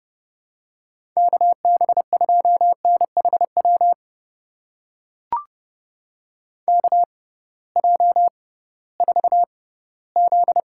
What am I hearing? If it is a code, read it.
Morse code, heard as a keyed tone, about 31 wpm, spaced as for 18 wpm, 700 Hz. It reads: KB2NHW KJ4Z
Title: USA Call Signs - 6 Characters - 6x Character Spacing - 30wpm